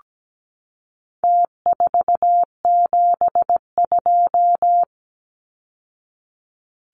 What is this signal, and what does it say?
Morse code, heard as a keyed tone, about 17 wpm, 700 Hz. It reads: T472
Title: Numbers - Contest Style - 4-Digit 17wpm